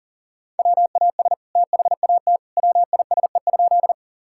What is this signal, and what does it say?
Morse code, heard as a keyed tone, about 37 wpm, 700 Hz. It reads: WAS THAT WISE?
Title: Sentences from Top 1600 Words 40wpm